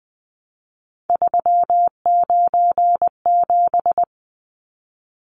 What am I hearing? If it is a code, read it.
Morse code, heard as a keyed tone, about 20 wpm, 700 Hz. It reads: 397
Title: Numbers - 3-Digit 20wpm